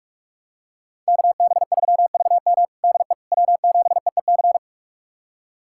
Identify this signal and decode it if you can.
Morse code, heard as a keyed tone, about 46 wpm, 700 Hz. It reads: KB3VM DE W7EEC
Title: USA Callsign DE Callsign - 5 Characters - 45wpm